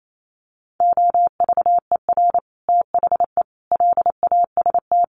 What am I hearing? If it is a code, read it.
Morse code, heard as a keyed tone, about 28 wpm, 700 Hz. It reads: OVER THE LAST
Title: Sets of 5 Words Encore 28wpm